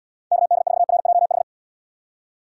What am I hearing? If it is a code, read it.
Morse code, heard as a keyed tone, about 78 wpm, 700 Hz. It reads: KD4RJH